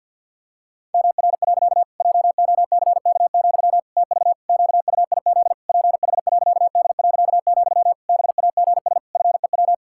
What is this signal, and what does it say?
Morse code, heard as a keyed tone, about 50 wpm, 700 Hz. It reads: MR. JOCK, TV QUIZ PH.D., BAGS FEW
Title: Pangram Challenge v2 - 50wpm